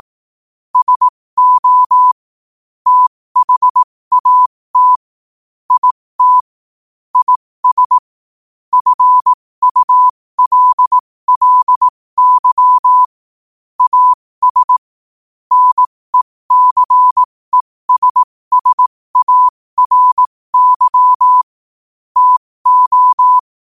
Morse code, heard as a keyed tone, about 18 wpm, spaced as for 15 wpm, 1000 Hz. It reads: SO THAT IT IS FULLY AS NECESSARY TO
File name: qotdfiles / 15wpm